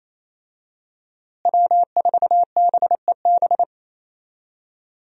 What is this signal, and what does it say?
Morse code, heard as a keyed tone, about 28 wpm, 700 Hz. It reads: W4BEB
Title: USA Call Signs - 5 Characters - Repeated 2x - 28wpm